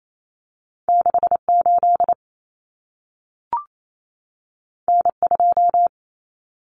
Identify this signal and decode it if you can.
Morse code, heard as a keyed tone, about 28 wpm, 700 Hz. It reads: 68 N2
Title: Common Cut Numbers - 2-Digit 28wpm